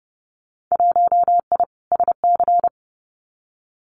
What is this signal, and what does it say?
Morse code, heard as a keyed tone, about 32 wpm, 700 Hz. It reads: 1I SC
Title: Winter Field Day Exchanges 30wpm